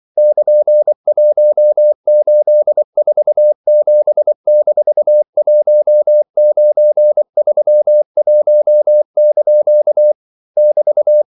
Morse code, heard as a keyed tone, about 24 wpm, 600 Hz. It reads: <KN>1847-1931) <BT>
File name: CW / Practice / EnglishQuotations0224